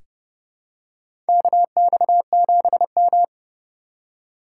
Morse code, heard as a keyed tone, about 30 wpm, 700 Hz. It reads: KX7M
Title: K1USN SST QSO Practice - 30wpm